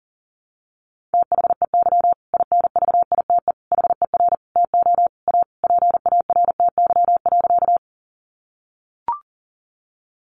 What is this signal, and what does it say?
Morse code, heard as a keyed tone, about 37 wpm, 700 Hz. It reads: THEY INVITE HER TO A PARTY.
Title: Easy Phrases 40wpm